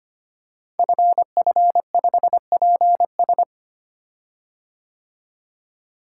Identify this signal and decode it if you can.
Morse code, heard as a keyed tone, about 25 wpm, 700 Hz. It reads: FF5PS
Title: Code Groups - 5 Letter:Numbers - 25wpm